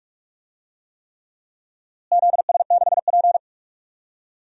Morse code, heard as a keyed tone, about 46 wpm, 690 Hz. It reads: ZS6P
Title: DXCC-100 Call Signs - 4 Characters - Repeated 3x - 45wpm